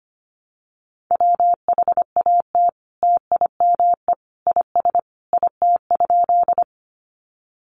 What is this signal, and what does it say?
Morse code, heard as a keyed tone, about 25 wpm, 700 Hz. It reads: WHAT TIME IS IT?
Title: Sentences from Top 100 Words 25wpm